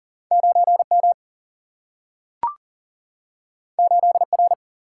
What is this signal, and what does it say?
Morse code, heard as a keyed tone, about 42 wpm, 700 Hz. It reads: 9 M 8 R
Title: Single-Letter Single-Number 40wpm